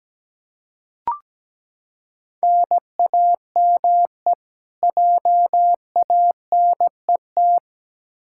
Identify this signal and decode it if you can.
Morse code, heard as a keyed tone, about 17 wpm, 700 Hz. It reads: NAME JANET
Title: K1USN SST QSO Practice 17wpm